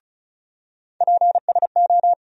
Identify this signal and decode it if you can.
Morse code, heard as a keyed tone, about 34 wpm, 700 Hz. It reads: PSO
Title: USA Call Sign Postfixes - 3 Letters 35wpm